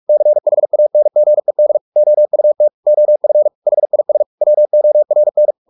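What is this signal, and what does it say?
Morse code, heard as a keyed tone, about 44 wpm, 600 Hz. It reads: CHANGED OUT OF HIS WORN
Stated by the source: Morse practice at 45 wpm, the novel Never by Ken Follett in English, from CW med Gnister.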